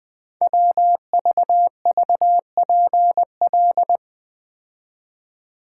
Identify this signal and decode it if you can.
Morse code, heard as a keyed tone, about 20 wpm, 700 Hz. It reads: WVVPL
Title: Code Groups - 5 Letter:Numbers - 20wpm